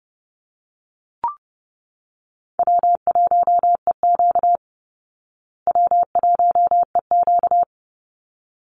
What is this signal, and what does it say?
Morse code, heard as a keyed tone, about 30 wpm, 700 Hz. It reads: W1EQ W1EQ
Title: Call Signs Repeated 3x - 30wpm